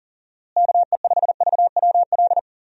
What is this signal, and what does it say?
Morse code, heard as a keyed tone, about 40 wpm, 700 Hz. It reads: KE5VWL
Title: USA Call Signs - 6 Characters - Repeated 2x - 40wpm